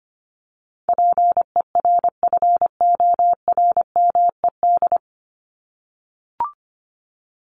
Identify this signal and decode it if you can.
Morse code, heard as a keyed tone, about 25 wpm, 700 Hz. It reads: PERFORMED
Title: ED Suffix - Top 100 Words with Base Words 25wpm